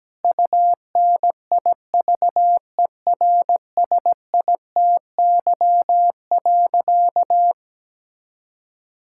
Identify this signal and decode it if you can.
Morse code, heard as a keyed tone, about 17 wpm, 700 Hz. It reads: UNIVERSITY.
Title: Sentences from Top 1000 Words - Review Part 5 - 17wpm